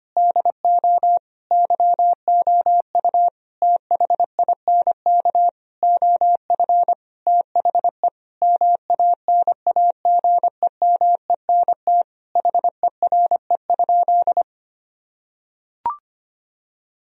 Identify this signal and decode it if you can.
Morse code, heard as a keyed tone, about 25 wpm, 700 Hz. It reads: DO YOU THINK OF THE MANAGEMENT HERE?
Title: Sentences from Top 1000 Words - Review Part 4 - 25wpm